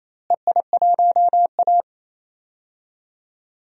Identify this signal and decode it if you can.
Morse code, heard as a keyed tone, about 28 wpm, 700 Hz. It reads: EI1A
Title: DXCC-100 Call Signs - 4 Characters - Repeated 3x - 28wpm